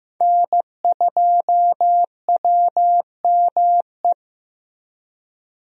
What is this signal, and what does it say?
Morse code, heard as a keyed tone, about 15 wpm, 700 Hz. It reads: N2WME
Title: USA Call Signs - 5 Characters - Repeated 2x - 15wpm